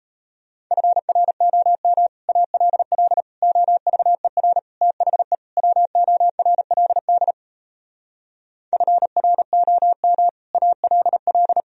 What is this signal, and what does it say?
Morse code, heard as a keyed tone, about 38 wpm, 700 Hz. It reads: FROM ALL OVER THE WORLD FROM ALL